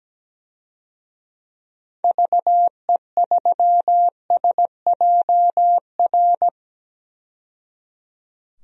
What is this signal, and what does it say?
Morse code, heard as a keyed tone, about 17 wpm, 700 Hz. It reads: VE3SJR
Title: DXCC-100 Call Signs - 6 Characters - 17wpm